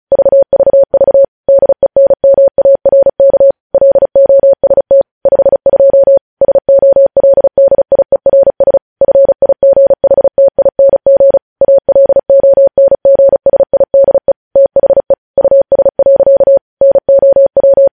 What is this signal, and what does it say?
Morse code, heard as a keyed tone, about 36 wpm, 550 Hz. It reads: VVV DENMARK LOST 52 SOLDIERS FIGHTING ALONGSIDE THE US. NOW
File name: podcast / News headlines-1x-35x35